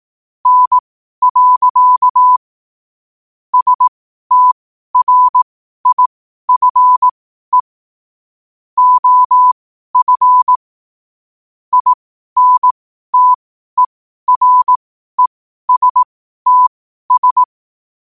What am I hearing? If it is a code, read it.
Morse code, heard as a keyed tone, about 18 wpm, spaced as for 12 wpm, 1000 Hz. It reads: N. STRIFE OF INTERESTS
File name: qotdfiles / 12wpm